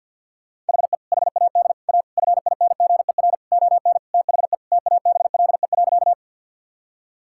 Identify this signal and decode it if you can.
Morse code, heard as a keyed tone, about 50 wpm, 700 Hz. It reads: HE HAD A FINGER ON THE TABLE.
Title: Sentences from Top 1600 Words 50wpm